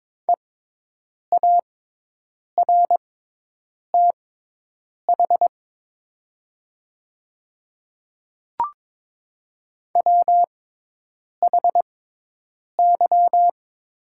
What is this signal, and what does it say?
Morse code, heard as a keyed tone, about 22 wpm, spaced as for 8 wpm, 700 Hz. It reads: EARTH WHY
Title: 6x Character Spacing - Top 500 Words - 22wpm